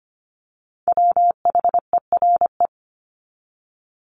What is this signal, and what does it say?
Morse code, heard as a keyed tone, about 25 wpm, 700 Hz. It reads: WHERE
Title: Top 300 Words 25wpm